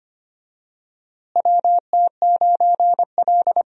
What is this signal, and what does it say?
Morse code, heard as a keyed tone, about 25 wpm, 700 Hz. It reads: WT9L
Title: USA Call Signs - 4 Characters - Repeated 3x - 25wpm